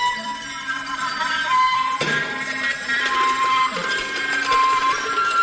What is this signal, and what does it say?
Emergency Alert System (SAME) tones, heard as harsh buzzing bursts. It is heard from 0.4 to 4.4 s.
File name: wrct-log hi- 21